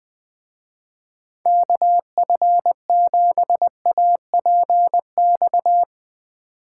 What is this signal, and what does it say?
Morse code, heard as a keyed tone, about 20 wpm, 700 Hz. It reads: KF7APX
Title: Speed Racing - USA 2x3 Call Signs 17wpm